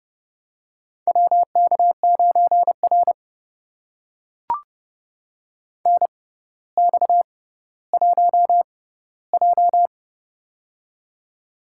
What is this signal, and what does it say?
Morse code, heard as a keyed tone, about 30 wpm, spaced as for 15 wpm, 700 Hz. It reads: WK9R NX1J
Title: USA Call Signs - 4 Characters - 6x Character Spacing - 30wpm